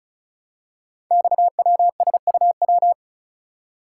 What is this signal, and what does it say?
Morse code, heard as a keyed tone, about 34 wpm, 700 Hz. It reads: XWSUW
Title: Code Groups - 5 Letters - 35wpm